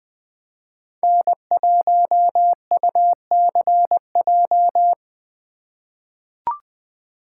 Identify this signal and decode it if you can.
Morse code, heard as a keyed tone, about 21 wpm, 700 Hz. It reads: N1UCJ
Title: USA Call Signs - 5 Characters - 2x Character Spacing - 20wpm